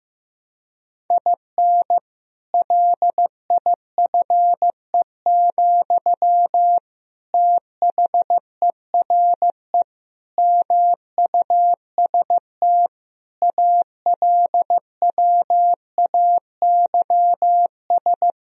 Morse code, heard as a keyed tone, about 14 wpm, 700 Hz. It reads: IN LIFE, THERE MUST ALWAYS
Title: Sentences from Top 1000 Words - Review Part 4 - 15wpm